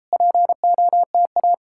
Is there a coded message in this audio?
Yes: Morse code from 0.1 to 1.5 s, heard as a keyed tone, about 38 wpm, 700 Hz. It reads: POTA